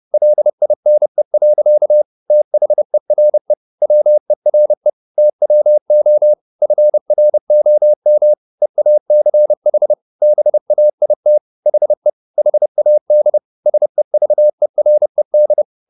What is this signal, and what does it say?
Morse code, heard as a keyed tone, about 30 wpm, 600 Hz. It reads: LINE. THERE WERE TWO FROM EACH BAIT HE HAD SEVERED
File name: the old man and the sea / the old man and the sea 1 30wpm